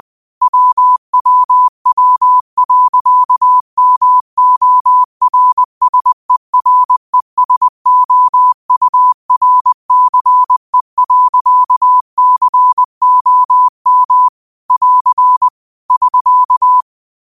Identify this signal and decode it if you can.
Morse code, heard as a keyed tone, about 20 wpm, 1000 Hz. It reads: WWW.MORSERESOURCE.COM <AR> <SK>
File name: 20wpm